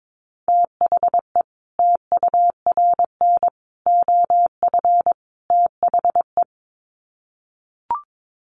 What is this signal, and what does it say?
Morse code, heard as a keyed tone, about 22 wpm, 700 Hz. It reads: THE TURN OF THE
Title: Sets of 4 Words 22wpm